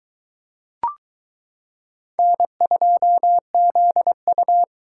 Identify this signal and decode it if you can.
Morse code, heard as a keyed tone, about 22 wpm, 700 Hz. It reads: N2ZU